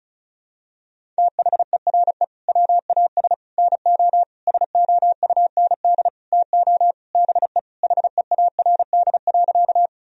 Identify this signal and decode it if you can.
Morse code, heard as a keyed tone, about 36 wpm, 700 Hz. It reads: THERE WAS NO SOUND TO BE HEARD.
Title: Sentences from Top 1000 Words - Review Part 1 - 35wpm